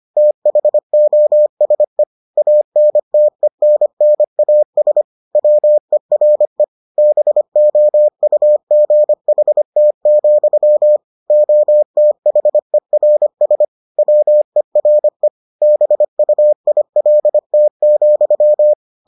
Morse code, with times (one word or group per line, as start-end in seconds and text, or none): 0.2-2.0 THOSE
2.4-5.0 ANTENNAS
5.4-6.6 WERE
7.0-11.0 BOUGHT,
11.3-13.7 OTHERS
14.0-15.3 WERE
15.6-18.7 BUILT,